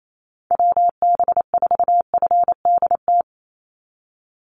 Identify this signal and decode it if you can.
Morse code, heard as a keyed tone, about 28 wpm, 700 Hz. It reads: WB4FDT
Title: Call Signs Repeated 2x - 28wpm